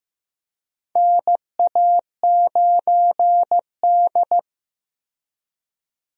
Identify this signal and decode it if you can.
Morse code, heard as a keyed tone, about 15 wpm, 700 Hz. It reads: NA9D
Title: USA Call Signs - 4 Characters - 15wpm